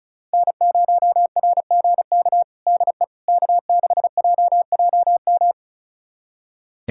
Morse code, heard as a keyed tone, about 36 wpm, 700 Hz. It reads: N0RGK DE K6JJM
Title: USA Callsign DE Callsign - 5 Characters - 35wpm